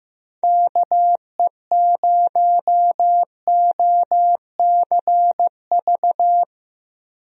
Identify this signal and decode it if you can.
Morse code, heard as a keyed tone, about 15 wpm, 700 Hz. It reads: KE0OCV
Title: USA Call Signs - 6 Characters - Repeated 2x - 15wpm